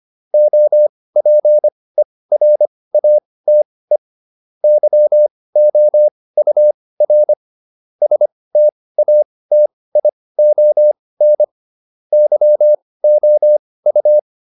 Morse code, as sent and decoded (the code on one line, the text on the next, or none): --- .--. .-. .- - . / -.-- --- ..- .-. / ... - .- - .. --- -. / -.-- --- ..-
OPERATE YOUR STATION YOU